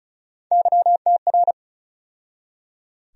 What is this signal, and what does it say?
Morse code, heard as a keyed tone, about 36 wpm, 700 Hz. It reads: YTR
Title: USA Call Sign Postfixes - 3 Letters 35wpm